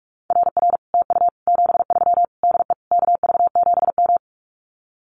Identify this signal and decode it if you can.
Morse code, heard as a keyed tone, about 46 wpm, 700 Hz. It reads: RR TU 73 DE KV7M
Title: Common QSO Phrases Encore 45wpm